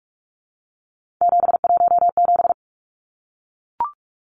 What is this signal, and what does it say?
Morse code, heard as a keyed tone, about 44 wpm, 700 Hz. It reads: 717
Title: Numbers - 3-Digit 45wpm